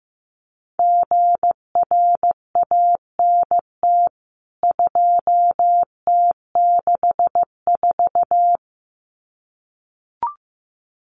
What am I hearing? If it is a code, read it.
Morse code, heard as a keyed tone, about 15 wpm, 700 Hz. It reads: GRANT 2T64
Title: CWT Exchange 15wpm